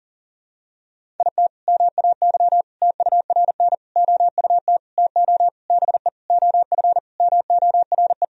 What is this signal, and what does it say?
Morse code, heard as a keyed tone, about 40 wpm, 700 Hz. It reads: IT MAY TURN OUT TO BE OF MORE
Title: Sentences from Top 400 Words and Abbreviations 40wpm